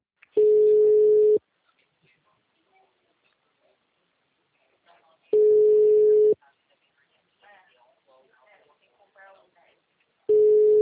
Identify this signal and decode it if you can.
Ringback tone (Europe, 425 Hz), with speech.
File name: out- -2010- - - .84317